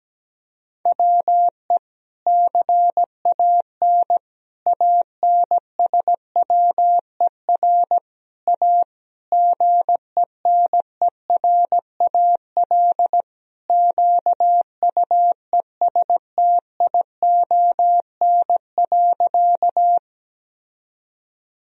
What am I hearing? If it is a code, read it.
Morse code, heard as a keyed tone, about 17 wpm, 700 Hz. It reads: WE CAN ANSWER A GENERAL QUESTION.
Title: Sentences from Top 1000 Words - Review Part 3 - 17wpm